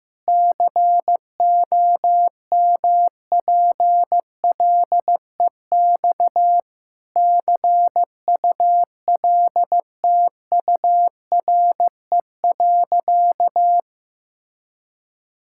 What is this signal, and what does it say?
Morse code, heard as a keyed tone, about 15 wpm, 700 Hz. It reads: COMPLEX CULTURE.
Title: Sentences from Top 1000 Words - Review Part 6 - 15wpm